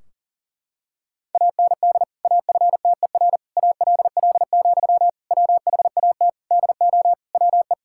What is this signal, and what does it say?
Morse code, heard as a keyed tone, about 40 wpm, 700 Hz. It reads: AND AFTER ALL, WHAT DO WE
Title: Sentences from Top 1600 Words 40wpm